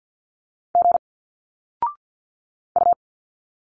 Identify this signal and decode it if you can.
Morse code, heard as a keyed tone, about 50 wpm, 700 Hz. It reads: G U
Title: Single Letters 50wpm